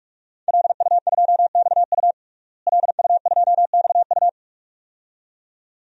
Morse code, heard as a keyed tone, about 46 wpm, 700 Hz. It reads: LU2XU LU2XU
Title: DXCC-100 Call Signs - 5 Characters - Repeated 2x - 45wpm